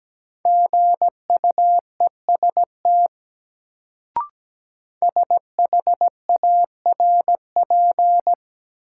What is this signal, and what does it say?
Morse code, heard as a keyed tone, about 17 wpm, 700 Hz. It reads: GUEST SHARP
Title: Sentences from Top 1600 Words - New Word Drill - 17wpm